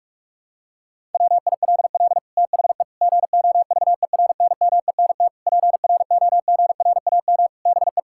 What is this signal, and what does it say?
Morse code, heard as a keyed tone, about 46 wpm, 700 Hz. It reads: WILL THE GOVERNMENT PROGRAM BE
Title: Sentences from Top 1000 Words - Review Part 5 - 45wpm